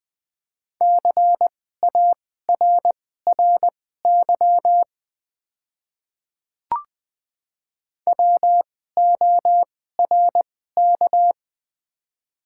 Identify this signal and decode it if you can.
Morse code, heard as a keyed tone, about 20 wpm, spaced as for 14 wpm, 700 Hz. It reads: CARRY WORK
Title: 2x Character Spacing - Top 500 Words - 20wpm